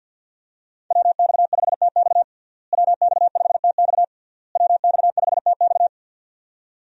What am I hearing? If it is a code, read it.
Morse code, heard as a keyed tone, about 52 wpm, 700 Hz. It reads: WX5TX WX5TX WX5TX
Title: USA Call Signs - 5 Characters - Repeated 3x - 50wpm